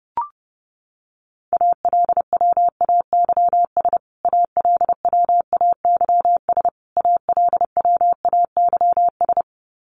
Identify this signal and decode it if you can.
Morse code, heard as a keyed tone, about 30 wpm, 700 Hz. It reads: ALWAYS ALWAYS ALWAYS
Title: Top 300 Words Repeated 30wpm